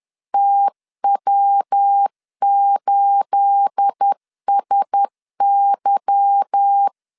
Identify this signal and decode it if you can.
Morse code, heard as a keyed tone, about 10 wpm, 800 Hz. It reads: TW8SY
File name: CW / DET / 1 9 1500C S A 10wpm 4511